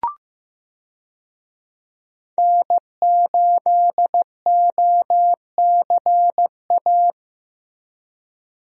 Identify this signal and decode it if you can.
Morse code, heard as a keyed tone, about 15 wpm, 700 Hz. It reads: N8OCA
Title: USA Call Signs - 5 Characters - Repeated 3x - 15wpm